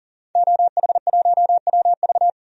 Morse code, heard as a keyed tone, about 40 wpm, 700 Hz. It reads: OH1WV
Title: DXCC-100 Call Signs - 5 Characters - 40wpm